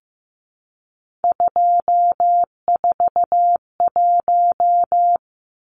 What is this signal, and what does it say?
Morse code, heard as a keyed tone, about 15 wpm, 700 Hz. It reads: 241
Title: Numbers - 3-Digit 15wpm